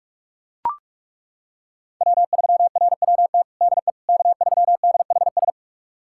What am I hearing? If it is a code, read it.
Morse code, heard as a keyed tone, about 42 wpm, 700 Hz. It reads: W3RWT DE K3DHS
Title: USA Callsign DE Callsign - 5 Characters - 45wpm